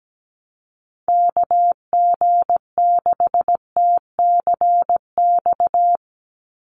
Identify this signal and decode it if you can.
Morse code, heard as a keyed tone, about 17 wpm, 700 Hz. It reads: KG6TCX
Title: USA Call Signs 17wpm